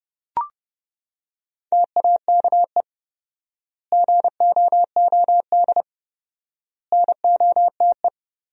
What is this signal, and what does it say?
Morse code, heard as a keyed tone, about 30 wpm, 700 Hz. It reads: TAKE GOOD NOTE
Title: 4x Word-Spacing - Sentences from Top 500 Words - 30wpm